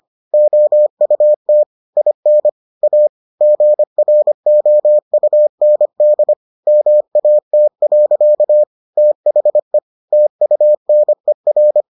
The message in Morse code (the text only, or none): OUT IN A GROUND MAT. THE TUNER